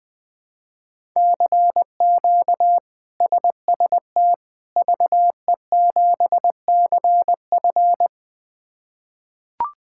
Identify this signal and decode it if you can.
Morse code, heard as a keyed tone, about 20 wpm, 700 Hz. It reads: CQ SST VE7CF
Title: K1USN SST QSO Practice - 20wpm